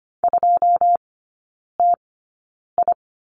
Morse code, heard as a keyed tone, about 25 wpm, spaced as for 9 wpm, 700 Hz. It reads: 2TI
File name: Lesson 14e - Number 2 - Three Characters 25wpm